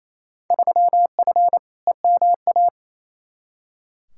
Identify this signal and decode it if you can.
Morse code, heard as a keyed tone, about 28 wpm, 700 Hz. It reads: 3F EMA
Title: Field Day Exchanges Encore - 28wpm